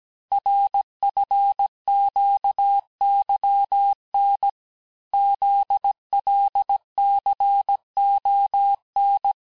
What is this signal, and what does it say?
Morse code, heard as a keyed tone, about 17 wpm, 800 Hz. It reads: RFQYN ZLCON
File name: corrige-cours-1581-texte-chiffre-vitesse-16-mpm